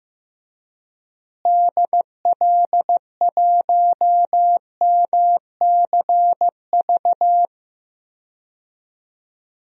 Morse code, heard as a keyed tone, about 16 wpm, 700 Hz. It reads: DL1MCV
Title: DXCC-100 Call Signs - 6 Characters - Repeated 3x - 15wpm